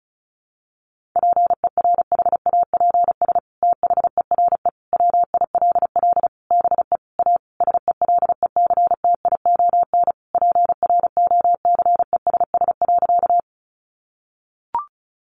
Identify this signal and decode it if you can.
Morse code, heard as a keyed tone, about 34 wpm, 700 Hz. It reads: PERHAPS THERE WILL BE A SELECTION PROCESS.
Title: Sentences from Top 1400 Words 35wpm